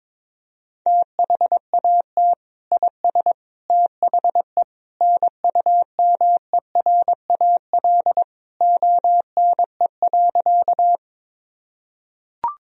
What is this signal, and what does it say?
Morse code, heard as a keyed tone, about 22 wpm, 700 Hz. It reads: THAT IS THE NUMERAL ONE.
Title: Sentences from Top 1000 Words - Review Part 2 - 22wpm